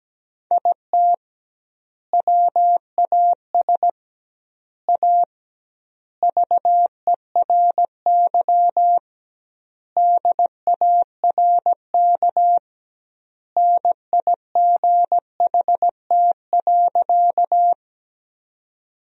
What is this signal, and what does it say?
Morse code, heard as a keyed tone, about 17 wpm, 700 Hz. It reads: IT WAS A VERY DARK NIGHT.
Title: 2x Word-Spacing - Sentences from Top 500 Words - 17wpm